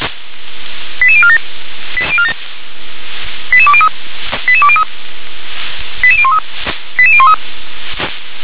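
Five-tone selective call (ZVEI), heard as quick runs of five tones, tones of about 70 ms. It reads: 88047 88203 88203 88013 88013
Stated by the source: Five-tone ZVEI selective call received by SDR.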